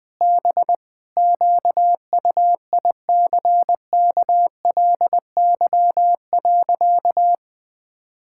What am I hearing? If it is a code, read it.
Morse code, heard as a keyed tone, about 21 wpm, 700 Hz. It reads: B QUICKLY.
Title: Sentences with 7-Word Phrases - 20wpm